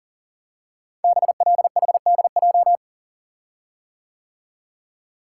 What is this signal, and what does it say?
Morse code, heard as a keyed tone, about 42 wpm, 700 Hz. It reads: BLHDJ
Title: Code Groups - 5 Letters - 40wpm